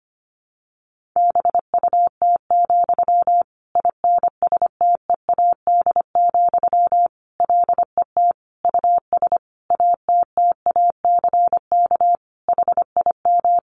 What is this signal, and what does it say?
Morse code, heard as a keyed tone, about 25 wpm, 700 Hz. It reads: BUT, INSTEAD, LET US ATTACK HIM
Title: Sentences from Top 1500 Words 25wpm